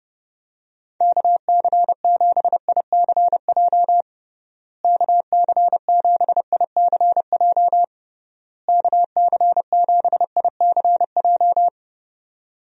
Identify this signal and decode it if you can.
Morse code, heard as a keyed tone, about 30 wpm, 700 Hz. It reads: KC7ICJ KC7ICJ KC7ICJ
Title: USA Call Signs - 6 Characters - Repeated 3x - 30wpm